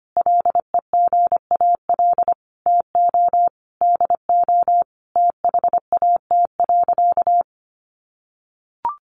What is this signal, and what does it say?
Morse code, heard as a keyed tone, about 25 wpm, 700 Hz. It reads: LEGAL TO DO THAT.
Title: Sentences from Top 1200 Words 25wpm